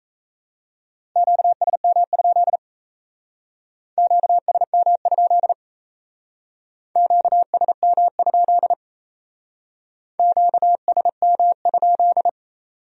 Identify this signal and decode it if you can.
Morse code, heard as a keyed tone, about 35 wpm, 700 Hz. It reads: QSM? QSM? QSM? QSM?
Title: Speed-Racing Q-Codes - 28wpm